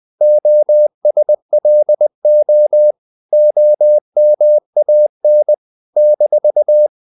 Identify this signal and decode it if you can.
Morse code, heard as a keyed tone, about 20 wpm, 600 Hz. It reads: OSLO OMAN -
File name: CW / Practice / NationalCapitals20